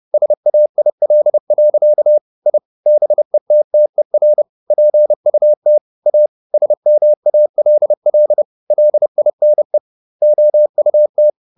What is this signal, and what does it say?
Morse code, heard as a keyed tone, about 30 wpm, 600 Hz. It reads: SAIL. I BETTER PUT A SMALL LINE OUT